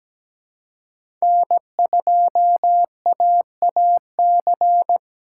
Morse code, heard as a keyed tone, about 17 wpm, 700 Hz. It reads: N2AAC
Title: USA Call Signs - 4x Character Spacing - 17wpm